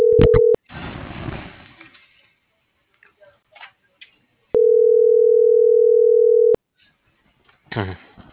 Ringback tone (North America), 440 + 480 Hz.
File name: CENTRAL MABE / out antonio.garcia - .56935